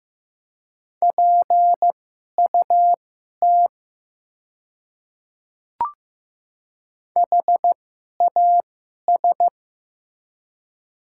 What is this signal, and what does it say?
Morse code, heard as a keyed tone, about 15 wpm, spaced as for 11 wpm, 700 Hz. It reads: PUT HAS
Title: 2x Character Spacing - Top 500 Words - 15wpm